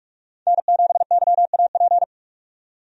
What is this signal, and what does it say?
Morse code, heard as a keyed tone, about 46 wpm, 700 Hz. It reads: N7YAP
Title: USA Call Signs - 5 Characters - 45wpm